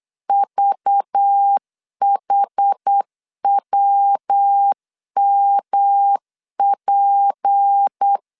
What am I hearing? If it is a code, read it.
Morse code, heard as a keyed tone, about 8 wpm, 800 Hz. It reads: VHWMP
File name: LEZIONI / 7 GF9Z / 7 GF9 8wpm 3545